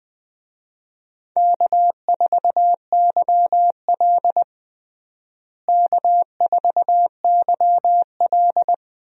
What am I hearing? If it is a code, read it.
Morse code, heard as a keyed tone, about 20 wpm, 700 Hz. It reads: K4YL K4YL
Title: USA Call Signs - 4 Characters - Repeated 2x - 20wpm